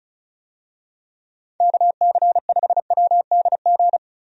Morse code, heard as a keyed tone, about 36 wpm, 700 Hz. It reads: KC5WDG
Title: USA Call Signs - 6 Characters - Repeated 3x - 35wpm